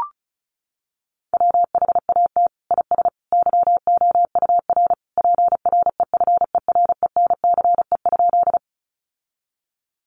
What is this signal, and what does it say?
Morse code, heard as a keyed tone, about 34 wpm, 700 Hz. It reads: WHAT IS YOUR PREFERENCE?
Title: Sentences from Top 1500 Words 35wpm